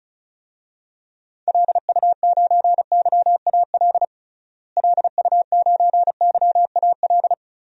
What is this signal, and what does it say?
Morse code, heard as a keyed tone, about 36 wpm, 700 Hz. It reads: LU9YAL LU9YAL
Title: DXCC-100 Call Signs - 6 Characters - Repeated 3x - 35wpm